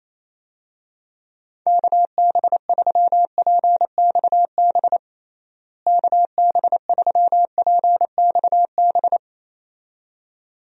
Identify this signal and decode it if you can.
Morse code, heard as a keyed tone, about 28 wpm, 700 Hz. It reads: KB3PXB KB3PXB
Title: USA Call Signs - 6 Characters - Repeated 2x - 28wpm